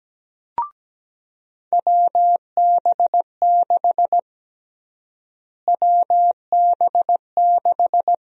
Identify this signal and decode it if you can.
Morse code, heard as a keyed tone, about 17 wpm, 700 Hz. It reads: WB6 WB6
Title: USA Call Sign Prefixes - Repeated 2x - 17wpm